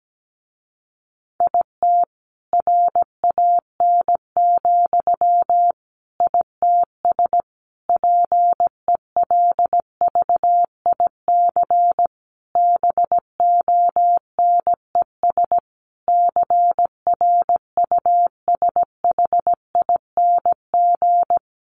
Morse code, heard as a keyed tone, about 17 wpm, 700 Hz. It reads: IT RAN, ITS PELVIC BONES CRUSHING
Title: A Sound Of Thunder 17wpm